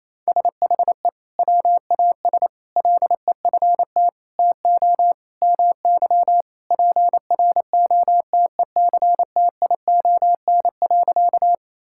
Morse code, heard as a keyed tone, about 28 wpm, 700 Hz. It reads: SHE WAS LEFT TO MY PROTECTION.
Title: Sentences from Top 1600 Words 28wpm